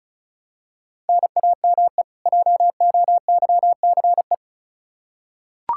Morse code, heard as a keyed tone, about 36 wpm, 700 Hz. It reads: NAME JOYCE
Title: K1USN SST QSO Practice 35wpm